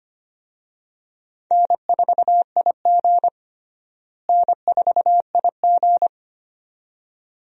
Morse code, heard as a keyed tone, about 25 wpm, 700 Hz. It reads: N4IG N4IG
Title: USA Call Signs - 4 Characters - Repeated 2x - 25wpm